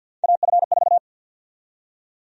Morse code, heard as a keyed tone, about 50 wpm, 700 Hz. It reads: AF4
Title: USA Call Sign Prefixes - 50wpm